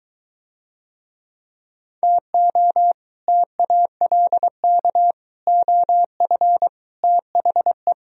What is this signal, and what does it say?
Morse code, heard as a keyed tone, about 23 wpm, 700 Hz. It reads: TO TALK OF THE